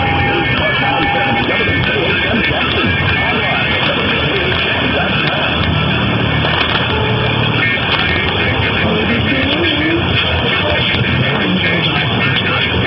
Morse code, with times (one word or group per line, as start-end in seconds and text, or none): none